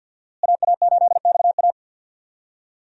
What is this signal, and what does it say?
Morse code, heard as a keyed tone, about 51 wpm, 700 Hz. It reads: AA8XA
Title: USA Call Signs - 5 Characters - Repeated 3x - 50wpm